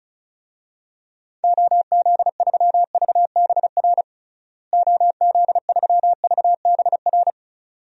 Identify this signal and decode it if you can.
Morse code, heard as a keyed tone, about 34 wpm, 700 Hz. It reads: OZ3VBR OZ3VBR
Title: DXCC-100 Call Signs - 6 Characters - Repeated 2x - 35wpm